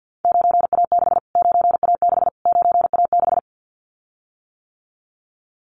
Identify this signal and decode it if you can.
Morse code, heard as a keyed tone, about 50 wpm, 700 Hz. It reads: 9A6 9A6 9A6
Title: Callsign Prefixes for Top 25 DXCC - Repeated 3x - 50wpm